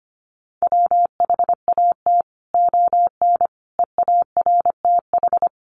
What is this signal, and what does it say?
Morse code, heard as a keyed tone, about 25 wpm, 700 Hz. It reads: WHAT ON EARTH
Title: Sentences from Top 200 Words 25wpm